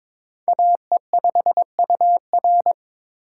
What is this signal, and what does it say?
Morse code, heard as a keyed tone, about 22 wpm, 700 Hz. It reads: AE5UR
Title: USA Call Signs - 5 Characters - 2x Character Spacing - 22wpm